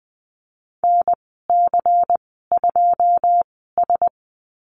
Morse code, heard as a keyed tone, about 20 wpm, spaced as for 15 wpm, 700 Hz. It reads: NC2S
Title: 140 Lesson 23g - Letter f - Call Signs 20wpm